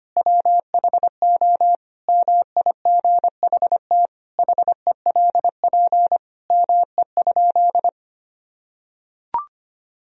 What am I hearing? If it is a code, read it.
Morse code, heard as a keyed tone, about 25 wpm, 700 Hz. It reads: WHO MIGHT HELP ME?
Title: Sentences from Top 1000 Words - Review Part 3 - 25wpm